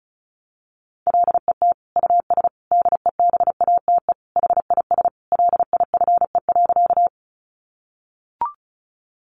Morse code, heard as a keyed tone, about 34 wpm, 700 Hz. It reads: LET US DEBATE HIS LIFE.
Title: Sentences from Top 1100 Words 35wpm